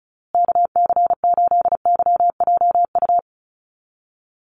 Morse code, heard as a keyed tone, about 34 wpm, 700 Hz. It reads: KC8YJU
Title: USA Call Signs Repeated 3x 35wpm